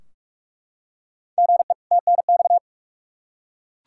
Morse code, heard as a keyed tone, about 46 wpm, 700 Hz. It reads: GE TNX
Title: Practice QSO - Exchange 1 - Beginning - 45wpm